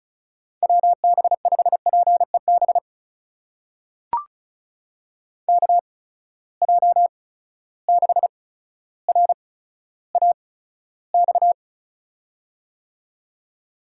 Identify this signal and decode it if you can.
Morse code, heard as a keyed tone, about 36 wpm, spaced as for 15 wpm, 700 Hz. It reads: WB5PEB KJ6RAX
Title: USA Call Signs - 8x Character Spacing - 35wpm